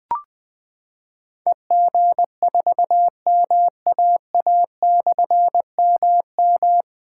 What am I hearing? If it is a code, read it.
Morse code, heard as a keyed tone, about 20 wpm, 700 Hz. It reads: EG4MAA/MM
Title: Difficult Call Signs 20wpm